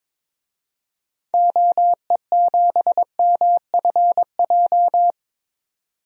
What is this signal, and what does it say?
Morse code, heard as a keyed tone, about 22 wpm, 700 Hz. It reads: OE7MFJ
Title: DXCC-100 Call Signs - 6 Characters - 22wpm